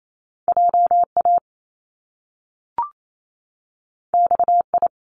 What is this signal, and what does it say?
Morse code, heard as a keyed tone, about 28 wpm, 700 Hz. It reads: JA XI
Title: Top Two Letter Combinations in Words 28wpm